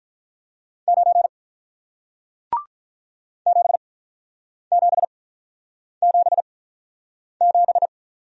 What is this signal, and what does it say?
Morse code, heard as a keyed tone, about 48 wpm, spaced as for 9 wpm, 690 Hz. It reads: 9 7777